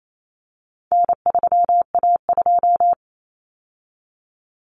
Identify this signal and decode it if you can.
Morse code, heard as a keyed tone, about 28 wpm, 700 Hz. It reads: N3A2
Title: Common Cut Numbers - 4-Digit 28wpm